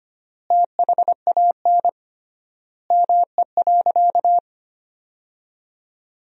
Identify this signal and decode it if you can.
Morse code, heard as a keyed tone, about 25 wpm, 700 Hz. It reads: THAN ME.
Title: 3x Word-Spacing - Sentences from Top 400 Words - 25wpm